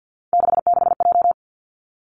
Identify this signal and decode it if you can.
Morse code, heard as a keyed tone, about 50 wpm, 700 Hz. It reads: 66J
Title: Three Challenging Letter-Number Combinations 50wpm